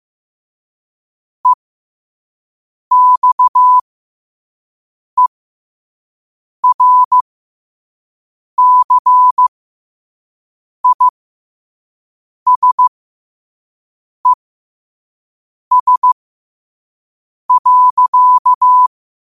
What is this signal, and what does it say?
Morse code, heard as a keyed tone, about 15 wpm, spaced as for 5 wpm, 1000 Hz. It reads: EXERCISES.